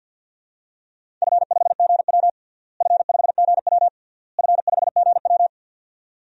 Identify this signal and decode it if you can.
Morse code, heard as a keyed tone, about 52 wpm, 700 Hz. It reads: F5GW F5GW F5GW
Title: DXCC-100 Call Signs - 4 Characters - Repeated 3x - 50wpm